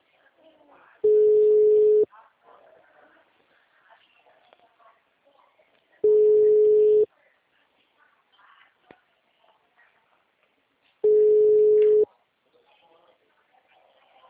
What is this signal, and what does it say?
Ringback tone (Europe, 425 Hz), with speech.
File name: out- -2003- - - .52672